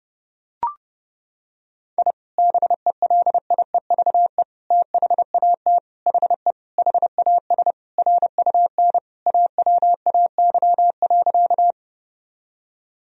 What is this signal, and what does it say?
Morse code, heard as a keyed tone, about 30 wpm, 700 Hz. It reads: I BELIEVE THAT HE HAS RUN AWAY.
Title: Sentences from Top 1000 Words - Review Part 1 - 30wpm